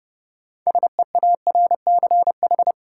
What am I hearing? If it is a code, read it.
Morse code, heard as a keyed tone, about 30 wpm, 700 Hz. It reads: SEARCH
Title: Sentences from Top 800 Words - New Word Drill - 30wpm